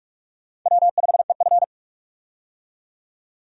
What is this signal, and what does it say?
Morse code, heard as a keyed tone, about 47 wpm, 700 Hz. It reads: W5EF
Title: USA Call Signs - 4 Characters - 45wpm